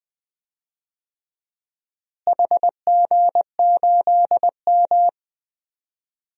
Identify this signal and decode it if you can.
Morse code, heard as a keyed tone, about 20 wpm, 700 Hz. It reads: HG8M
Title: DXCC-100 Call Signs - 4 Characters - Repeated 3x - 20wpm